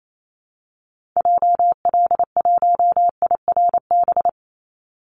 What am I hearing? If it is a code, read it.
Morse code, heard as a keyed tone, about 28 wpm, 700 Hz. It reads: JL1IRB
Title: Call Signs Repeated 3x - 28wpm